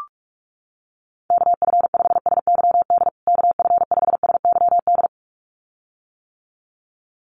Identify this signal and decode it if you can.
Morse code, heard as a keyed tone, about 44 wpm, 700 Hz. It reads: KF5SYD KF5SYD
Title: USA Call Signs Repeated 2x 45wpm